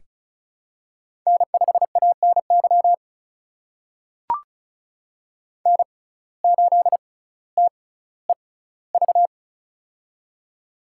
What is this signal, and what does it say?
Morse code, heard as a keyed tone, about 36 wpm, spaced as for 18 wpm, 700 Hz. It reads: N5ANY N8TEV
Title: USA Call Signs - 6x Character Spacing - 35wpm